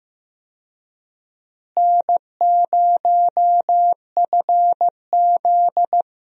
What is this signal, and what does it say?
Morse code, heard as a keyed tone, about 15 wpm, 700 Hz. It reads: N0FZ
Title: USA Call Signs - 4 Characters - Repeated 2x - 15wpm